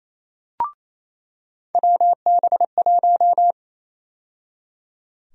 Morse code, heard as a keyed tone, about 28 wpm, 700 Hz. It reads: WB1
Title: USA Call Sign Prefixes - 28wpm